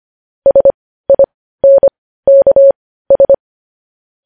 Morse code, heard as a keyed tone, about 25 wpm, spaced as for 15 wpm, 550 Hz. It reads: SINKS
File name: podcast / News headlines-1x-25x15